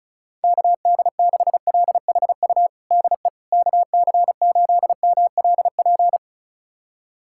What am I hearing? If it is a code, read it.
Morse code, heard as a keyed tone, about 36 wpm, 700 Hz. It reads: KD6LHU DE KC8MLP
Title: USA Callsign DE Callsign 35wpm